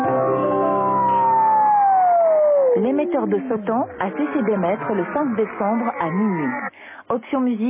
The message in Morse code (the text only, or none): none